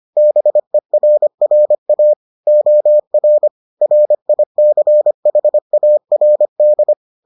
Morse code, heard as a keyed tone, about 25 wpm, 600 Hz. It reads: BERRA OR RICHARD